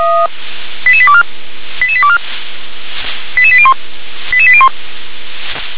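Five-tone selective call (ZVEI), heard as quick runs of five tones, tones of about 70 ms. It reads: 88824 88824 88801 88801
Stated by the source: ZVEI five-tone call 88832 received on SDR.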